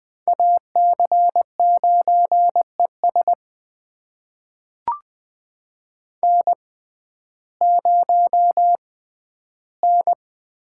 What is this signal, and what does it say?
Morse code, heard as a keyed tone, about 20 wpm, spaced as for 12 wpm, 700 Hz. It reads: AC9ES N0N
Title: USA Call Signs - 5 Characters - 6x Character Spacing - 20wpm